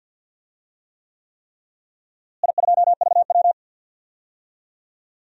Morse code, heard as a keyed tone, about 52 wpm, 700 Hz. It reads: I2VW
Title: DXCC-100 Call Signs - 4 Characters - Repeated 3x - 50wpm